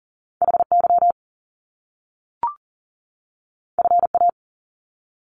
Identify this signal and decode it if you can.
Morse code, heard as a keyed tone, about 40 wpm, 700 Hz. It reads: HY FA
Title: Top Two Letter Combinations in Words 40wpm